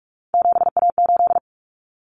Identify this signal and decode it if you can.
Morse code, heard as a keyed tone, about 44 wpm, 700 Hz. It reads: 7A8
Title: Common Cut Numbers - 3-Digit 45wpm